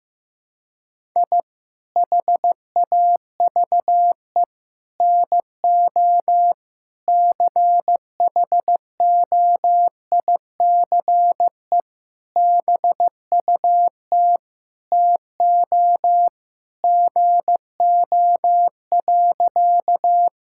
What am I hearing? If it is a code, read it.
Morse code, heard as a keyed tone, about 15 wpm, 700 Hz. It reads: I HAVE NO CHOICE BUT TO GO.
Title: Sentences from Top 1000 Words - Review Part 6 - 15wpm